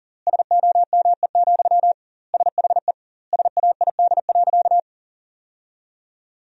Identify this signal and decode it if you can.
Morse code, heard as a keyed tone, about 40 wpm, 700 Hz. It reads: SOME, SHE SAID.
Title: 2x Word-Spacing - Sentences from Top 200 Words - 40wpm